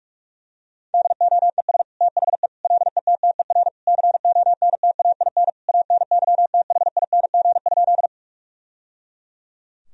Morse code, heard as a keyed tone, about 42 wpm, 700 Hz. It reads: DOES THE LETTER CONTAIN ANYTHING?
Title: Sentences from Top 1000 Words - Review Part 2 - 45wpm